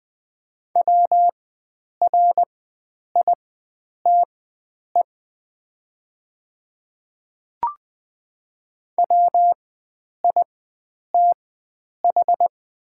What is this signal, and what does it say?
Morse code, heard as a keyed tone, about 20 wpm, spaced as for 9 wpm, 700 Hz. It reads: WRITE WITH
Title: 4x Character Spacing - Top 100 Words - 20wpm